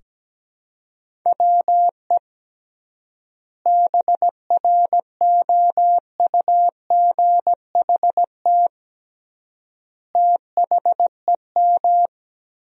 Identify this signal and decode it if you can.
Morse code, heard as a keyed tone, about 17 wpm, 700 Hz. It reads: WE BROUGHT THEM
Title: 3x Word-Spacing - Sentences from Top 500 Words - 17wpm